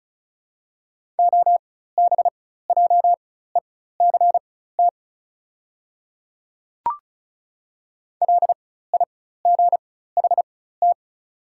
Morse code, heard as a keyed tone, about 36 wpm, spaced as for 16 wpm, 690 Hz. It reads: OBJECT LIGHT
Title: 4x Character Spacing - Top 500 Words - 35wpm